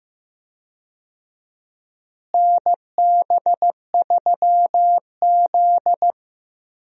Morse code, heard as a keyed tone, about 15 wpm, 700 Hz. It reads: NB3Z
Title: USA Call Signs - 4 Characters - Repeated 3x - 15wpm